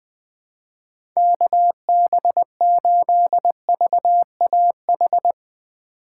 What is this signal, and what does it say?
Morse code, heard as a keyed tone, about 20 wpm, 700 Hz. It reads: KB8VAH
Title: USA Call Signs - 2x Character Spacing - 20wpm